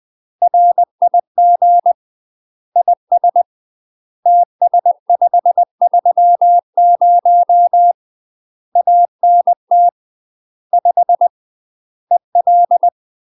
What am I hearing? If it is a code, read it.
Morse code, heard as a keyed tone, about 20 wpm, 700 Hz. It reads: RIG IS TS530 ANT 5 EL
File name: UK MASTER QSOs TRAINING / 20WPM / UK MASTER QSOs 0014